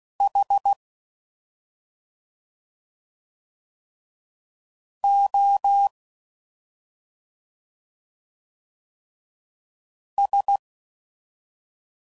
Morse code, heard as a keyed tone, about 16 wpm, 800 Hz. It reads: H O S